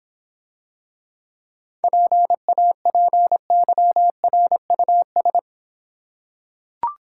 Morse code, heard as a keyed tone, about 26 wpm, 700 Hz. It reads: PAPYRUS